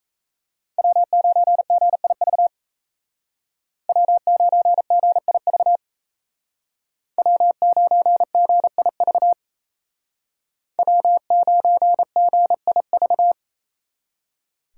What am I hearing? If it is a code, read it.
Morse code, heard as a keyed tone, about 36 wpm, 700 Hz. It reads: W9GIV W9GIV W9GIV W9GIV